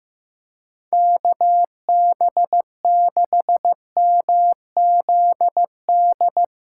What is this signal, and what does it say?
Morse code, heard as a keyed tone, about 16 wpm, 700 Hz. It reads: KB6MZD
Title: USA Callsign DE Callsign - 6 Characters - 15wpm